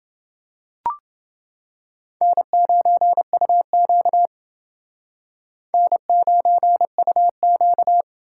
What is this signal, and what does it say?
Morse code, heard as a keyed tone, about 29 wpm, 700 Hz. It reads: N9UQ N9UQ